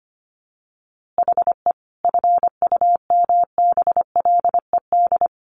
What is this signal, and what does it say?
Morse code, heard as a keyed tone, about 25 wpm, 700 Hz. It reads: HE FUMBLED
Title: A Sound Of Thunder 25wpm